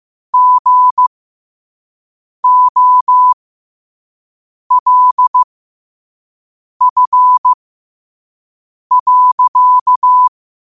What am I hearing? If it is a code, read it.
Morse code, heard as a keyed tone, about 15 wpm, spaced as for 5 wpm, 1000 Hz. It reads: GOLF.
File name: 2024-10-07-5wpm